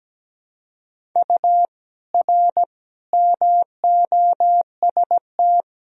Morse code, heard as a keyed tone, about 17 wpm, 700 Hz. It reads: U R MOST
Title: Sentences from Top 200 Words and Abbreviations 17wpm